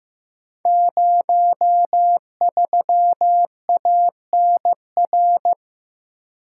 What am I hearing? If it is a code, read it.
Morse code, heard as a keyed tone, about 15 wpm, 700 Hz. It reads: 03ANR
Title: Code Groups - 5 Letter:Numbers - 15wpm